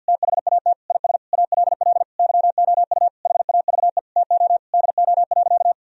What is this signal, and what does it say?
Morse code, heard as a keyed tone, about 50 wpm, 700 Hz. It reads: THAT IS ALL YOU HAVE TO DO.
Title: Sentences from Top 1000 Words - Review Part 2 - 50wpm